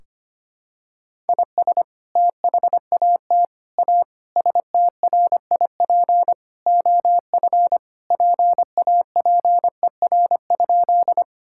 Morse code, heard as a keyed tone, about 25 wpm, 700 Hz. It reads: IS THAT A STRIP OF PAPER?
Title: Sentences from Top 1600 Words 25wpm